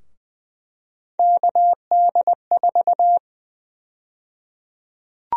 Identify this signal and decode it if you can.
Morse code, heard as a keyed tone, about 20 wpm, 700 Hz. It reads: KD4
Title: USA Call Sign Prefixes - 20wpm